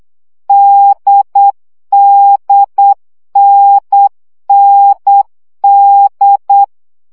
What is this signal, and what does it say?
Morse code, heard as a keyed tone, about 8 wpm, 800 Hz. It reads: DDNND
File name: CW gyakorlas n d k 7 baud